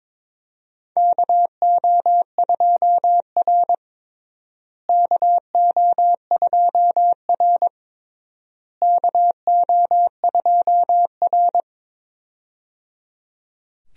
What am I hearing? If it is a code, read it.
Morse code, heard as a keyed tone, about 22 wpm, 700 Hz. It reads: KO2R KO2R KO2R
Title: USA Call Signs - 4 Characters - Repeated 3x - 22wpm